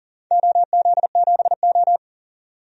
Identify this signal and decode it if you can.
Morse code, heard as a keyed tone, about 40 wpm, 700 Hz. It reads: OZ7O
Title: DXCC-100 Call Signs - 4 Characters - 40wpm